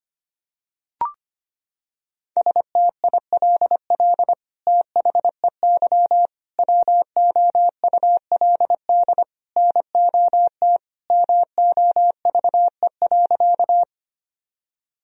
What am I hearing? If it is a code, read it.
Morse code, heard as a keyed tone, about 25 wpm, 700 Hz. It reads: STILL THEY WOULD NOT MOVE.
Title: Sentences from Top 1000 Words - Review Part 1 - 25wpm